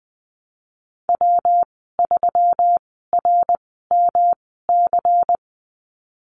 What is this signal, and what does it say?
Morse code, heard as a keyed tone, about 20 wpm, spaced as for 15 wpm, 700 Hz. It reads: W3RMC
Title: 147 Lesson 24g - Letter y - Call Signs 20wpm